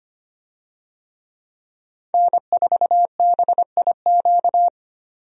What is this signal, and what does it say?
Morse code, heard as a keyed tone, about 25 wpm, 700 Hz. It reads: N4BIQ
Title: USA Call Signs - 5 Characters - 25wpm